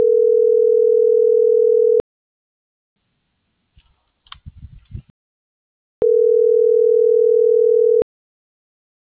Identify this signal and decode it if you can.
Ringback tone (North America), 440 + 480 Hz.